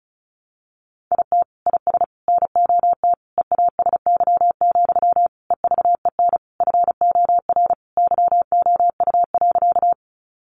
Morse code, heard as a keyed tone, about 34 wpm, 700 Hz. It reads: IT IS NOT EASY, EVEN FOR YOU.
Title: Sentences from Top 700 Words 35wpm